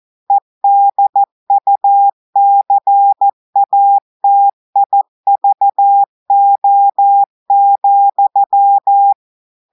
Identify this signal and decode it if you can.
Morse code, heard as a keyed tone, about 14 wpm, 800 Hz. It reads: EDUCATIVO,
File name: audios / portugais / portugais texte en clair 1 vitesse 14 mots minutes